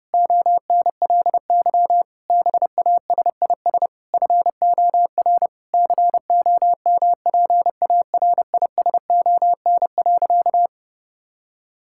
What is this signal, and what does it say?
Morse code, heard as a keyed tone, about 30 wpm, 700 Hz. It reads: ONLY BASIS FOR COMPARISON.
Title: Sentences from Top 1600 Words 30wpm